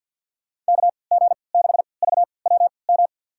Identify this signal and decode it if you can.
Morse code, heard as a keyed tone, about 52 wpm, spaced as for 29 wpm, 700 Hz. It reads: KG6VWM
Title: USA Call Signs - 6 Characters - 3x Character Spacing - 50wpm